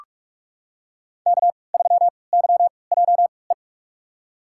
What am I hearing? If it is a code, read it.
Morse code, heard as a keyed tone, about 42 wpm, spaced as for 25 wpm, 700 Hz. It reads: K3YJE
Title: USA Call Signs - 5 Characters - 3x Character Spacing - 45wpm